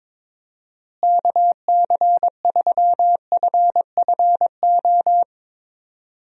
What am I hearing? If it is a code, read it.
Morse code, heard as a keyed tone, about 21 wpm, 700 Hz. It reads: KC3FFO
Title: USA Call Signs - 6 Characters - Repeated 3x - 22wpm